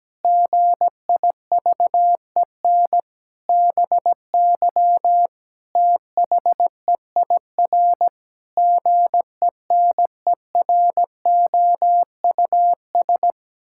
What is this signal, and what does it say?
Morse code, heard as a keyed tone, about 17 wpm, 700 Hz. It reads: GIVEN BY THEIR GENEROUS
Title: Pangram Challenge v2 - 17wpm